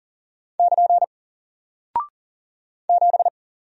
Morse code, heard as a keyed tone, about 40 wpm, 700 Hz. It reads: <KN> 7
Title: Single Character (Letter:Number:Punctuation:Prosign) Alt 40wpm